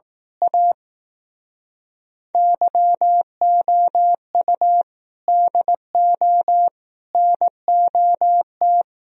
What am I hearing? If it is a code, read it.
Morse code, heard as a keyed tone, about 18 wpm, 700 Hz. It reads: A YOU DO NOT
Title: Speed-Racing - Sets of 5 Words 15wpm